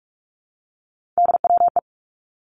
Morse code, heard as a keyed tone, about 46 wpm, 700 Hz. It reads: DWE
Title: Three Challenging Letter-Number Combinations 45wpm